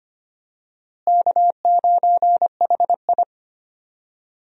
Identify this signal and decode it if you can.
Morse code, heard as a keyed tone, about 25 wpm, 700 Hz. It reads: K9HI
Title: USA Call Signs - 4 Characters - Repeated 2x - 25wpm